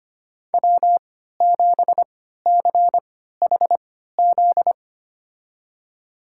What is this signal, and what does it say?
Morse code, heard as a keyed tone, about 25 wpm, spaced as for 14 wpm, 700 Hz. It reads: W7CHZ
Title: USA Call Signs - 5 Characters - 3x Character Spacing - 25wpm